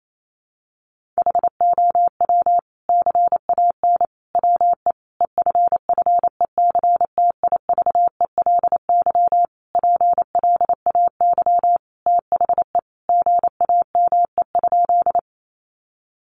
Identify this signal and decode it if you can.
Morse code, heard as a keyed tone, about 28 wpm, 700 Hz. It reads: HOW CAN WE EFFECTIVELY PLAY THE GAME?
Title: Sentences from Top 1400 Words 28wpm